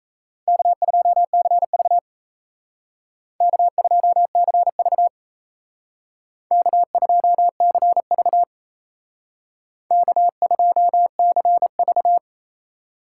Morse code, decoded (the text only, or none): K2CV K2CV K2CV K2CV